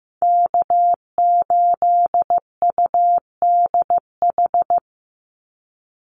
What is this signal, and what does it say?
Morse code, heard as a keyed tone, about 15 wpm, 700 Hz. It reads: K8UDH
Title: Call Signs Repeated 3x - 15wpm